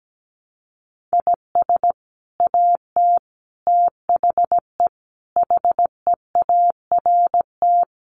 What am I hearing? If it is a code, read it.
Morse code, heard as a keyed tone, about 17 wpm, 700 Hz. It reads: IS AT THE HEART
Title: Sets of 5 Words Encore2 17wpm